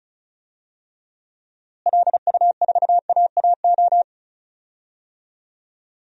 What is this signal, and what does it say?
Morse code, heard as a keyed tone, about 34 wpm, 700 Hz. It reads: LU4AAO
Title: DXCC-100 Call Signs - 6 Characters - Repeated 3x - 35wpm